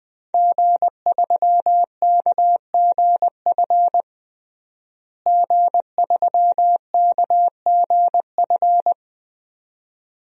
Morse code, heard as a keyed tone, about 20 wpm, 700 Hz. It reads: G3KGF G3KGF
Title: DXCC-100 Call Signs - 5 Characters - Repeated 2x - 20wpm